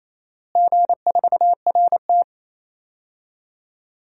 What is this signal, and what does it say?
Morse code, heard as a keyed tone, about 28 wpm, 700 Hz. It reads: G4RT
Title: DXCC-100 Call Signs - 4 Characters - Repeated 2x - 28wpm